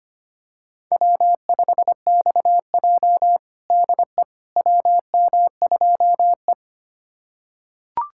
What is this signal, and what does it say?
Morse code, heard as a keyed tone, about 25 wpm, 700 Hz. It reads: W5XJ DE WM2E
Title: USA Callsign DE Callsign - 4 Characters - 25wpm